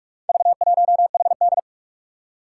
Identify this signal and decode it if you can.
Morse code, heard as a keyed tone, about 44 wpm, 700 Hz. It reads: V1HD
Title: Four Challenging Letter-Number Combinations 45wpm